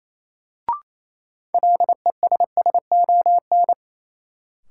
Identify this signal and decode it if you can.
Morse code, heard as a keyed tone, about 28 wpm, 700 Hz. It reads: LESSON
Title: Sentences from Top 1200 Words - New Word Drill - 28wpm